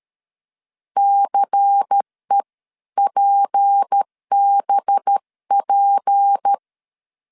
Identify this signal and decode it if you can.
Morse code, heard as a keyed tone, about 13 wpm, 800 Hz. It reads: CEPBP
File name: LEZIONI / 6 PKX0 / 6 P 12wpm 2033